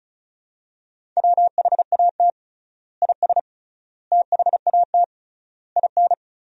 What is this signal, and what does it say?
Morse code, heard as a keyed tone, about 34 wpm, 700 Hz. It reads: WHAT IS THAT IN
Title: 3x Word-Spacing - Sentences from Top 200 Words - 35wpm